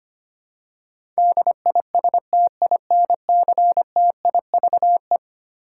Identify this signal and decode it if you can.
Morse code, heard as a keyed tone, about 25 wpm, 700 Hz. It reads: DISTINCTIVE
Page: morse-code-ninja-practice-25wpm